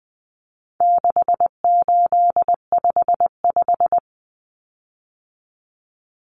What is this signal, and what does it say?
Morse code, heard as a keyed tone, about 20 wpm, 700 Hz. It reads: 6855
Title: Common Cut Numbers - 4-Digit 20wpm